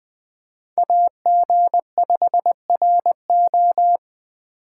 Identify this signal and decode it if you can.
Morse code, heard as a keyed tone, about 24 wpm, 700 Hz. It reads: AG5RO
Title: Speed Racing - USA Call Signs 20wpm